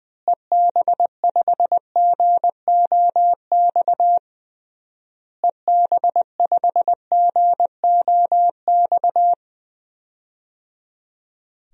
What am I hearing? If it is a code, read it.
Morse code, heard as a keyed tone, about 20 wpm, 700 Hz. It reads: EB5GOX EB5GOX
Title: DXCC-100 Call Signs - 6 Characters - Repeated 3x - 20wpm